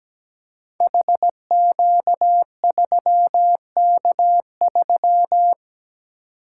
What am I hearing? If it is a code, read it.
Morse code, heard as a keyed tone, about 17 wpm, 700 Hz. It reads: HQ3K3
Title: Code Groups - 5 Letter:Numbers - 17wpm